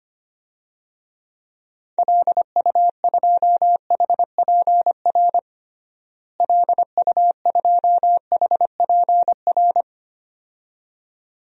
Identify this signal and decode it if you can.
Morse code, heard as a keyed tone, about 25 wpm, 700 Hz. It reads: LU2HPR LU2HPR
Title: DXCC-100 Call Signs - 6 Characters - Repeated 2x - 25wpm